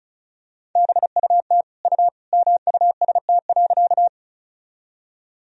Morse code, heard as a keyed tone, about 36 wpm, 700 Hz. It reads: BUT U MUST.
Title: Sentences from Top 200 Words and Abbreviations 35wpm